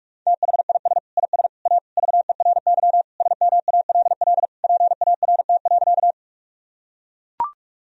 Morse code, heard as a keyed tone, about 41 wpm, 700 Hz. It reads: THIS IS A VERY SMALL PART.
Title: Sentences from Top 1000 Words - Review Part 1 - 45wpm